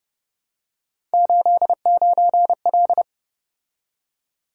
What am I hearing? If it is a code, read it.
Morse code, heard as a keyed tone, about 32 wpm, 700 Hz. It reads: 89L